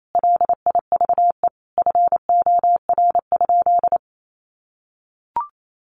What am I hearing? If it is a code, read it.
Morse code, heard as a keyed tone, about 28 wpm, 700 Hz. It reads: LIVE FOR?
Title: Sentences from Top 200 Words 28wpm